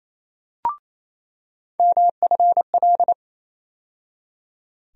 Morse code, heard as a keyed tone, about 28 wpm, 700 Hz. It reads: MFL
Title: USA Call Sign Postfixes - 3 Letters 28wpm